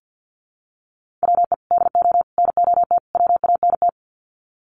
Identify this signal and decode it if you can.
Morse code, heard as a keyed tone, about 50 wpm, 700 Hz. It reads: WE DO NOT WANT
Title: Sets of 4 Words 50wpm